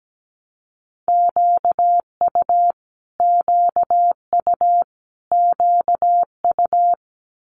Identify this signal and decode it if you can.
Morse code, heard as a keyed tone, about 17 wpm, 700 Hz. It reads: QU QU QU
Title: Multi-letter Phonemes Repeated 3x 17wpm